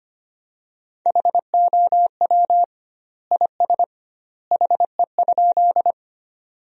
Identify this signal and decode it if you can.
Morse code, heard as a keyed tone, about 25 wpm, 700 Hz. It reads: HOW IS HE?
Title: 2x Word-Spacing - Sentences from Top 100 Words - 25wpm